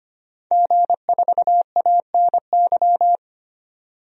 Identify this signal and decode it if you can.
Morse code, heard as a keyed tone, about 25 wpm, 700 Hz. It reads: G4ANY
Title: DXCC-100 Call Signs - 5 Characters - 25wpm